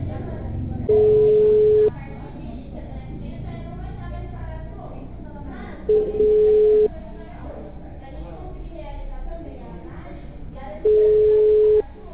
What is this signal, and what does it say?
Ringback tone (Europe, 425 Hz), with speech.